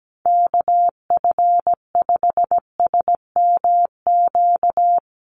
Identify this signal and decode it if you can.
Morse code, heard as a keyed tone, about 17 wpm, 700 Hz. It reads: KF5SMQ
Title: USA Call Signs Repeated 3x 17wpm